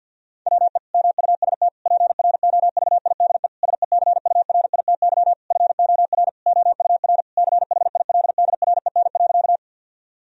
Morse code, heard as a keyed tone, about 50 wpm, 700 Hz. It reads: WE MUST PROVIDE SECURITY FOR OUR CHILDREN.
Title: Sentences from Top 1000 Words - Review Part 4 - 50wpm